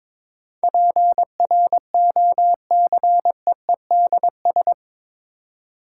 Morse code, heard as a keyed tone, about 22 wpm, 700 Hz. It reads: PROCEEDS
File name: practice / 22wpm / Eight Letter Words - 22wpm